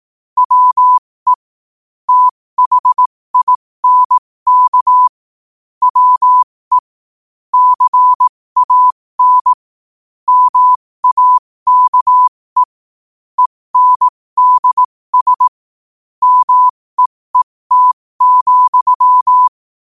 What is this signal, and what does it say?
Morse code, heard as a keyed tone, about 18 wpm, spaced as for 15 wpm, 1000 Hz. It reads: WE THINK WE CAN MAKE ENDS MEET,
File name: qotdfiles / 15wpm